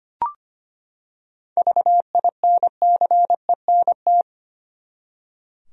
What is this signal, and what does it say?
Morse code, heard as a keyed tone, about 25 wpm, 700 Hz. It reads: VINCENT
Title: USA Common Names - 25wpm